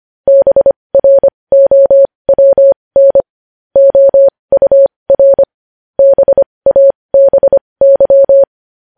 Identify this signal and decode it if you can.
Morse code, heard as a keyed tone, about 25 wpm, spaced as for 20 wpm, 550 Hz. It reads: BROWN OUR BABY
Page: news headlines in morse code at 20 wpm